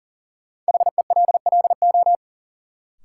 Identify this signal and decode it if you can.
Morse code, heard as a keyed tone, about 40 wpm, 700 Hz. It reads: HELLO
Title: Sentences from Top 1100 Words - New Word Drill Alt - 40wpm